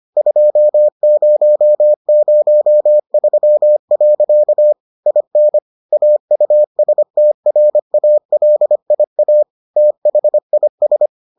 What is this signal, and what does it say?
Morse code, heard as a keyed tone, about 25 wpm, 600 Hz. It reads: 2003. IN AUSTRALIA THIS